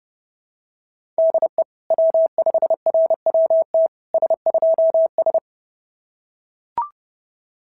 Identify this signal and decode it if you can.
Morse code, heard as a keyed tone, about 28 wpm, 660 Hz. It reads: DE W5RWT S2S
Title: SOTA QSO Practice 30wpm